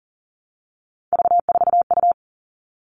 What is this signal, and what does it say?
Morse code, heard as a keyed tone, about 40 wpm, 700 Hz. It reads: V4U
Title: All Cut Numbers - 3-Digit 40wpm